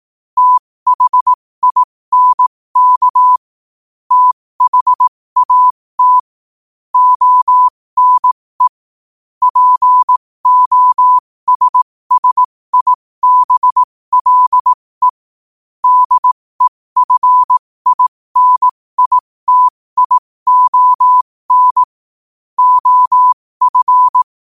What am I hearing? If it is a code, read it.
Morse code, heard as a keyed tone, about 18 wpm, spaced as for 15 wpm, 1000 Hz. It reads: THINK THAT ONE POSSIBLE DEFINITION OF
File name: qotdfiles / 15wpm